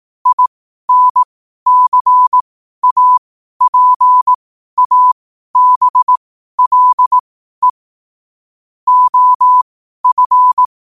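Morse code, heard as a keyed tone, about 18 wpm, spaced as for 12 wpm, 1000 Hz. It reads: INCAPABLE OF
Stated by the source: Morse code at 12 wpm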